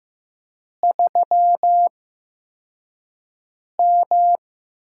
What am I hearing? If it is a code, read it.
Morse code, heard as a keyed tone, about 15 wpm, spaced as for 4 wpm, 700 Hz. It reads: 3M
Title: USA Call Signs - 8x Character Spacing - 15wpm